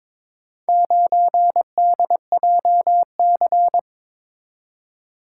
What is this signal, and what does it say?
Morse code, heard as a keyed tone, about 22 wpm, 700 Hz. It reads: 9DJC
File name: practice / 22wpm / Four Letter-Number Combinations - 22wpm